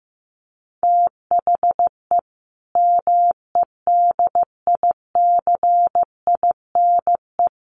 Morse code, heard as a keyed tone, about 15 wpm, 700 Hz. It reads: THE MEDICINE
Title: Sentences from Top 900 Words 15wpm